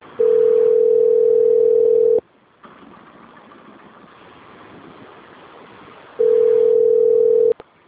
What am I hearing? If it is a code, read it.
Ringback tone (North America), 440 + 480 Hz.